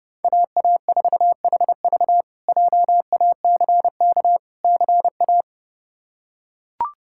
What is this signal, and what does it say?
Morse code, heard as a keyed tone, about 30 wpm, 700 Hz. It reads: AA4HV JACK CA
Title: K1USN SST QSO Practice - 30wpm